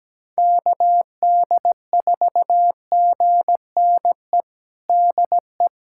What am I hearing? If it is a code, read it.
Morse code, heard as a keyed tone, about 17 wpm, 700 Hz. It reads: KD4GNE DE
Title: USA Callsign DE Callsign 17wpm